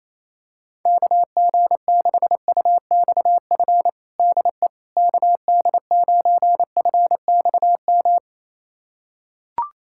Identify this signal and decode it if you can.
Morse code, heard as a keyed tone, about 28 wpm, 700 Hz. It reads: KG6UXF DE KD9FXM
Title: USA Callsign DE Callsign 28wpm